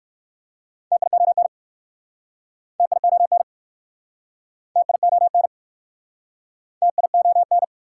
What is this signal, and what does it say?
Morse code, heard as a keyed tone, about 60 wpm, 700 Hz. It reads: TION TION TION TION